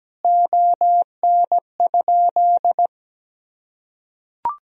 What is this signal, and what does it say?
Morse code, heard as a keyed tone, about 17 wpm, 700 Hz. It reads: ON?
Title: Sentences from Top 1000 Words - Review Part 3 - 17wpm